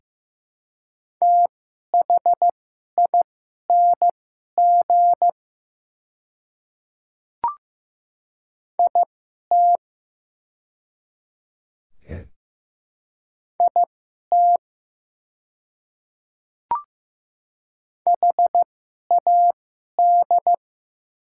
Morse code, heard as a keyed tone, about 15 wpm, spaced as for 11 wpm, 700 Hz. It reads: THING IT IT HAD
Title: 2x Character Spacing - Top 200 Words - 15wpm